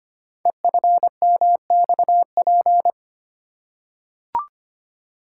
Morse code, heard as a keyed tone, about 25 wpm, 700 Hz. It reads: EFMXP
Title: Code Groups - 5 Letters - 25wpm